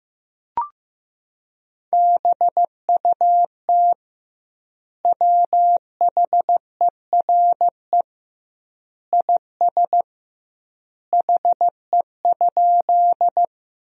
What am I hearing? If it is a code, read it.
Morse code, heard as a keyed tone, about 15 wpm, 700 Hz. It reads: BUT WHERE IS HE?
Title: 2x Word-Spacing - Sentences from Top 200 Words - 15wpm